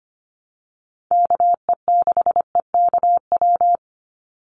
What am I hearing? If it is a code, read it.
Morse code, heard as a keyed tone, about 25 wpm, 700 Hz. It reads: KE6EKW
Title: USA Call Signs 25wpm